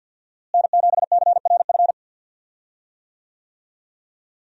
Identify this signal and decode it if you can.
Morse code, heard as a keyed tone, about 50 wpm, 700 Hz. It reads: N7CRF
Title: USA Call Signs - 5 Characters - Repeated 3x - 50wpm